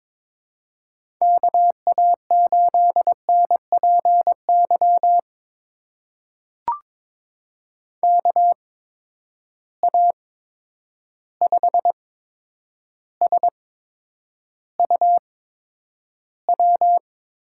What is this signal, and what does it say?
Morse code, heard as a keyed tone, about 22 wpm, spaced as for 9 wpm, 700 Hz. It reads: KA8NPY KA5SUW
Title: USA Call Signs - 6 Characters - 8x Character Spacing - 22wpm